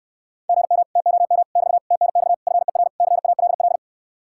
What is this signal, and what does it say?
Morse code, heard as a keyed tone, about 69 wpm, 700 Hz. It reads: GM TOM <BT> TNX FER CALL